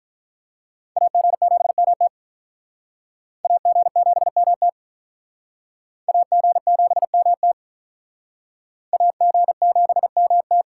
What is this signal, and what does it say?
Morse code, heard as a keyed tone, about 46 wpm, 700 Hz. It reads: AG7MT AG7MT AG7MT AG7MT